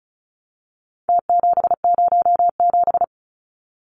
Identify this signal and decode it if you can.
Morse code, heard as a keyed tone, about 34 wpm, 700 Hz. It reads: T707
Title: Numbers - Contest Style - 4-Digit 35wpm